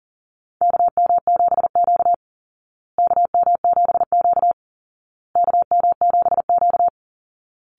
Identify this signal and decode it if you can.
Morse code, heard as a keyed tone, about 40 wpm, 700 Hz. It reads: KM7Q KM7Q KM7Q
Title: Call Signs Repeated 2x - 40wpm